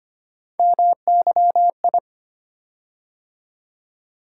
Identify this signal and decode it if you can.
Morse code, heard as a keyed tone, about 25 wpm, 700 Hz. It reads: MYI
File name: Three Letter-Number Combinations - 25wpm